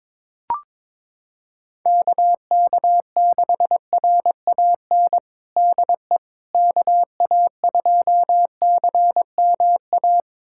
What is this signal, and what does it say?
Morse code, heard as a keyed tone, about 22 wpm, 700 Hz. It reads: KK6RAN DE KA2CMA
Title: USA Callsign DE Callsign - 6 Characters - 22wpm